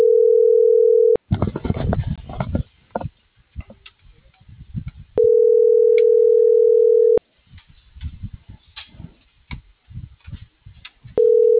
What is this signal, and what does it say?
Ringback tone (North America), 440 + 480 Hz.